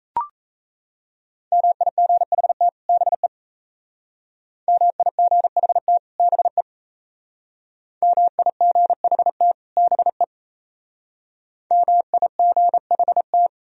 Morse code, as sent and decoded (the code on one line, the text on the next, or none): -- .. --. .... - / -... / -- .. --. .... - / -... / -- .. --. .... - / -... / -- .. --. .... -
MIGHT BE MIGHT BE MIGHT BE MIGHT